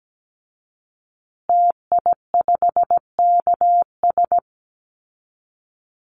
Morse code, heard as a keyed tone, about 17 wpm, 700 Hz. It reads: TI5KS
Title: Call Signs Repeated 3x - 17wpm